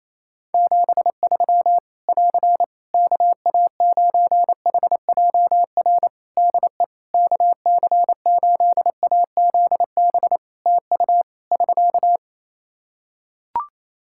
Morse code, heard as a keyed tone, about 28 wpm, 700 Hz. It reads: 73 <AR> KA9HJR DE KC8AZB TU <SK>
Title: Practice QSO - Exchange 4 - Ending - 28wpm